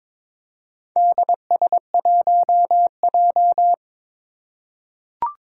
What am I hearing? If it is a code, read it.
Morse code, heard as a keyed tone, about 22 wpm, 700 Hz. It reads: DS1J
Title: Four Challenging Letter-Number Combinations 22wpm